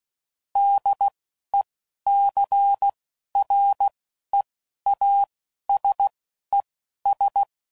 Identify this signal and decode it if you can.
Morse code, heard as a keyed tone, about 16 wpm, spaced as for 11 wpm, 800 Hz. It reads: DECREASES
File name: CW Code Groups @10WPM / Technical Subjects / XL